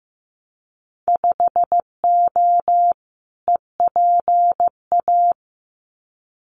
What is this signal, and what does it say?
Morse code, heard as a keyed tone, about 15 wpm, 700 Hz. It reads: 5O EPA
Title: Winter Field Day Exchanges Encore 15wpm